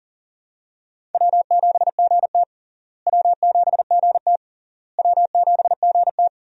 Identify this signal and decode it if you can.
Morse code, heard as a keyed tone, about 40 wpm, 700 Hz. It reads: W7GT W7GT W7GT
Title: USA Call Signs - 4 Characters - Repeated 3x - 40wpm